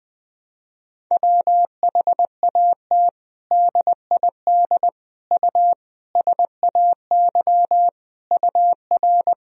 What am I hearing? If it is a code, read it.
Morse code, heard as a keyed tone, about 20 wpm, 700 Hz. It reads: WHAT DID U SAY UR
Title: Sentences from Top 200 Words and Abbreviations 20wpm